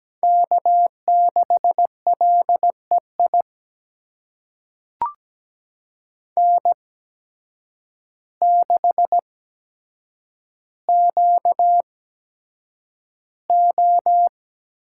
Morse code, heard as a keyed tone, about 17 wpm, spaced as for 8 wpm, 700 Hz. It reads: K6LEI N6QO
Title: USA Call Signs - 5 Characters - 8x Character Spacing - 17wpm